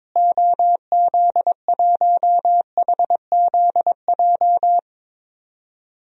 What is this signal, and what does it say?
Morse code, heard as a keyed tone, about 22 wpm, 700 Hz. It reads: OZ1HZJ
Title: DXCC-100 Call Signs - 6 Characters - Repeated 3x - 22wpm